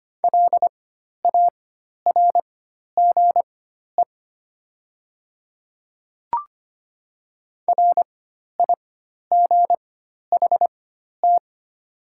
Morse code, heard as a keyed tone, about 25 wpm, spaced as for 12 wpm, 690 Hz. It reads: LARGE RIGHT
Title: 4x Character Spacing - Top 200 Words - 25wpm